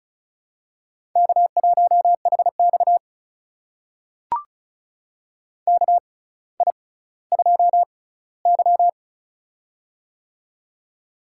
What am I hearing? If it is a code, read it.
Morse code, heard as a keyed tone, about 36 wpm, spaced as for 18 wpm, 700 Hz. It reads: K1HX KI2Y
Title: USA Call Signs - 4 Characters - 6x Character Spacing - 35wpm